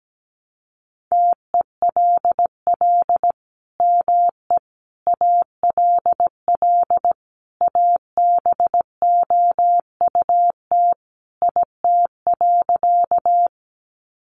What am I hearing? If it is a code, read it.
Morse code, heard as a keyed tone, about 17 wpm, 700 Hz. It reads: TELL ME ALL ABOUT IT.
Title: Sentences from Top 200 Words 17wpm